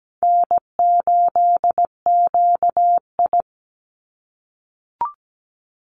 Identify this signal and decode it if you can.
Morse code, heard as a keyed tone, about 17 wpm, 700 Hz. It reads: N8QI
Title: USA Call Signs 17wpm